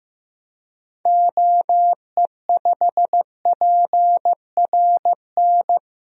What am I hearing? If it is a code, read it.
Morse code, heard as a keyed tone, about 15 wpm, 700 Hz. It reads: OE5PRN
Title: DXCC-100 Call Signs - 6 Characters - Repeated 3x - 15wpm